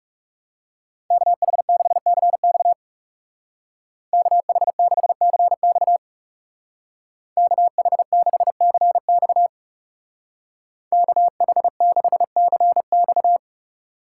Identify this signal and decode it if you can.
Morse code, heard as a keyed tone, about 38 wpm, 700 Hz. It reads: KH6CX KH6CX KH6CX KH6CX